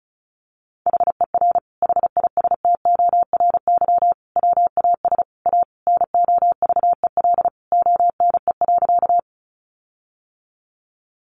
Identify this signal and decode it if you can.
Morse code, heard as a keyed tone, about 34 wpm, 700 Hz. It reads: HER HISTORY WAS A NOVEL ONE.
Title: Sentences from Top 1300 Words 35wpm